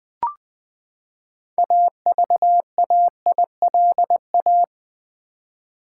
Morse code, heard as a keyed tone, about 20 wpm, 700 Hz. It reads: AVAILA
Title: Build A Word - 6 Letters At A Time - 12-Letter Words - 20wpm